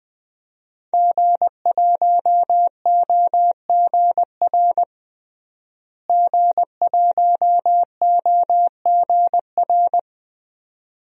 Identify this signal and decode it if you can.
Morse code, heard as a keyed tone, about 20 wpm, 700 Hz. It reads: G1OGR G1OGR
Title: DXCC-100 Call Signs - 5 Characters - Repeated 2x - 20wpm